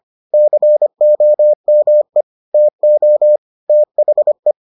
Morse code, heard as a keyed tone, about 25 wpm, 600 Hz. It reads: COME TO THE